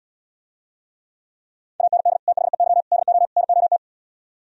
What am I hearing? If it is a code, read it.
Morse code, heard as a keyed tone, about 78 wpm, 700 Hz. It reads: AND THEY DO NOT